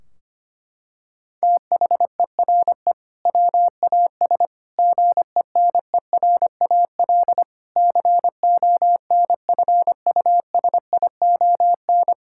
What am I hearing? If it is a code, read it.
Morse code, heard as a keyed tone, about 25 wpm, 700 Hz. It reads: THERE WAS GENERAL CONFUSION
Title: Sentences from Top 1600 Words 25wpm